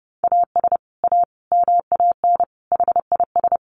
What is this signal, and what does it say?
Morse code, heard as a keyed tone, about 30 wpm, 700 Hz. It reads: AS A MAN HIS
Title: Sentences from Top 1200 Words 30wpm